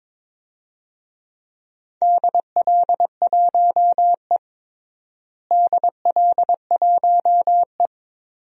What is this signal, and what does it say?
Morse code, heard as a keyed tone, about 22 wpm, 700 Hz. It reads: DL1E DL1E
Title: DXCC-100 Call Signs - 4 Characters - Repeated 2x - 22wpm